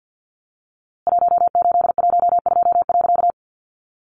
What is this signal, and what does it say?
Morse code, heard as a keyed tone, about 50 wpm, 700 Hz. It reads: 1812.
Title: A Sound Of Thunder 50wpm